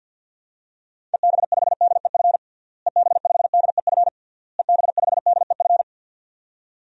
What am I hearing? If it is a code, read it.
Morse code, heard as a keyed tone, about 51 wpm, 700 Hz. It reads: EB5DEF EB5DEF EB5DEF
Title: DXCC-100 Call Signs - 6 Characters - Repeated 3x - 50wpm